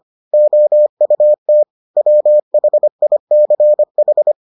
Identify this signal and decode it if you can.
Morse code, heard as a keyed tone, about 25 wpm, 600 Hz. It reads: OUT WHICH